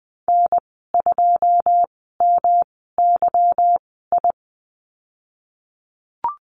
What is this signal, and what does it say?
Morse code, heard as a keyed tone, about 20 wpm, spaced as for 15 wpm, 700 Hz. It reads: N2MYI
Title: 173 Lesson 28g - Number 9 - Call Signs 20wpm